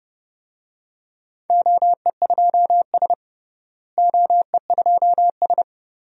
Morse code, heard as a keyed tone, about 30 wpm, 700 Hz. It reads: OE2S OE2S
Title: DXCC-100 Call Signs - 4 Characters - Repeated 3x - 30wpm